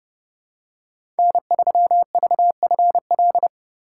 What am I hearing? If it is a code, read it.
Morse code, heard as a keyed tone, about 31 wpm, 700 Hz. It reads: N3VFL
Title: USA Call Signs - 5 Characters - 8x Character Spacing - 30wpm